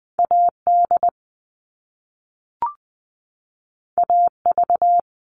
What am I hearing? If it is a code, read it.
Morse code, heard as a keyed tone, about 20 wpm, 700 Hz. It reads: A D A V
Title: All Cut Numbers - 2-Digit 20wpm